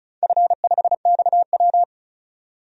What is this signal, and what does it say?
Morse code, heard as a keyed tone, about 36 wpm, 700 Hz. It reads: F5XW
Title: DXCC-100 Call Signs - 4 Characters - 35wpm